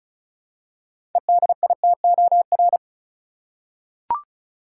Morse code, heard as a keyed tone, about 36 wpm, 700 Hz. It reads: EDITOR
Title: Six Letter Words - 35wpm